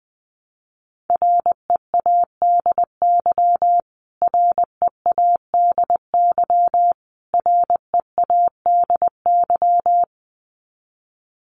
Morse code, heard as a keyed tone, about 20 wpm, 700 Hz. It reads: READY READY READY
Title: Top 500 Words Repeated 20wpm